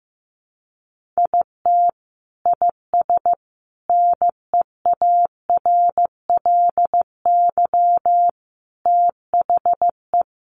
Morse code, heard as a keyed tone, about 15 wpm, 700 Hz. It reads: IT IS NEARLY THE
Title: Sentences from Top 1300 Words 15wpm